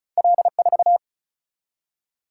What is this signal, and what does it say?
Morse code, heard as a keyed tone, about 36 wpm, 700 Hz. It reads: L4
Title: Single-Letter Single-Number 35wpm